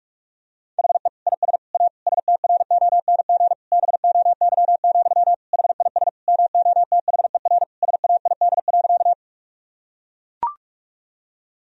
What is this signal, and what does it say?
Morse code, heard as a keyed tone, about 41 wpm, 700 Hz. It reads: HE IS A STRONG BOY, HIS MOTHER SAID.
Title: Sentences from Top 1000 Words - Review Part 3 - 45wpm